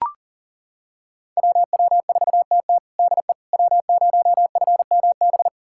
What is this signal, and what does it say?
Morse code, heard as a keyed tone, about 40 wpm, 700 Hz. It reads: WW4TT DE W0FMB
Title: USA Callsign DE Callsign - 5 Characters - 40wpm